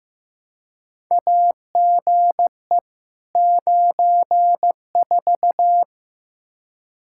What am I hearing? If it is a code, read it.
Morse code, heard as a keyed tone, about 16 wpm, 700 Hz. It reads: AGE 94
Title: K1USN SST QSO Practice 15wpm